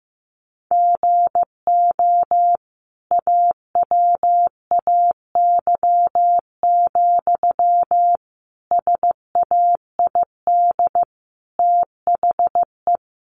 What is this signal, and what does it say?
Morse code, heard as a keyed tone, about 15 wpm, 700 Hz. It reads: GO AWAY, SAID THE
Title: Sentences from Top 1500 Words 15wpm